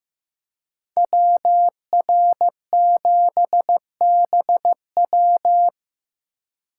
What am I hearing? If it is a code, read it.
Morse code, heard as a keyed tone, about 18 wpm, 700 Hz. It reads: WR7BW
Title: Speed Racing - USA 2x2 Call Signs 15wpm